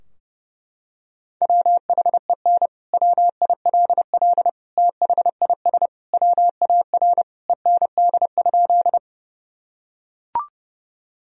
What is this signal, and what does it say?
Morse code, heard as a keyed tone, about 30 wpm, 700 Hz. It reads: WHEN WILL THIS WAR END?
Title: Sentences from Top 1000 Words - Review Part 2 - 30wpm